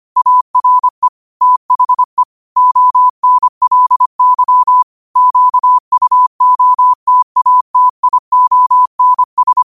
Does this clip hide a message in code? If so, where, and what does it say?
Yes: Morse code from 0.2 to 9.6 s, heard as a keyed tone, about 25 wpm, 1000 Hz. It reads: ARE THE ONLY QUOTATIONS